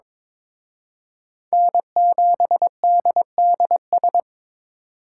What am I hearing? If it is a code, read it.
Morse code, heard as a keyed tone, about 26 wpm, 700 Hz. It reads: N7DDS
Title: Speed Racing - USA Call Signs 22wpm